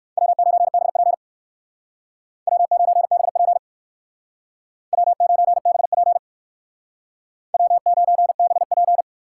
Morse code, heard as a keyed tone, about 59 wpm, 700 Hz. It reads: W9BP W9BP W9BP W9BP